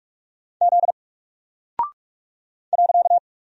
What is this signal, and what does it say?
Morse code, heard as a keyed tone, about 46 wpm, 700 Hz. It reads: Z .
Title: Single Character (Letter:Number:Punctuation:Prosign) 45wpm